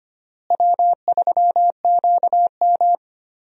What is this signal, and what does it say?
Morse code, heard as a keyed tone, about 25 wpm, 700 Hz. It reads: W3QM
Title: USA Call Signs - 4 Characters - Repeated 3x - 25wpm